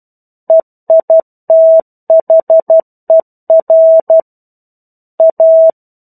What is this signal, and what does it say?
Morse code, heard as a keyed tone, about 12 wpm, 650 Hz. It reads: EITHER A
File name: 01-Chapter-01 12 WPM